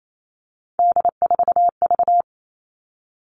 Morse code, heard as a keyed tone, about 28 wpm, 700 Hz. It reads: D4V
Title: All Cut Numbers - 3-Digit 28wpm